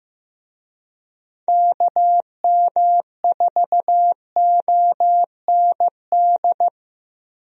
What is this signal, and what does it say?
Morse code, heard as a keyed tone, about 15 wpm, 700 Hz. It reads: KM4OND
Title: USA Call Signs - 6 Characters - 15wpm